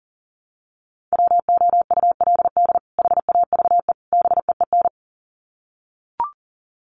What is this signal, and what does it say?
Morse code, heard as a keyed tone, about 40 wpm, 700 Hz. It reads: WOULD HAVE BEEN
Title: Sets of 3 Words 40wpm